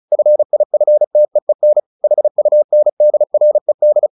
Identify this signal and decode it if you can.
Morse code, heard as a keyed tone, about 35 wpm, 600 Hz. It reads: FIFTEEN HUNDRED
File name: the old man and the sea 2 35wpm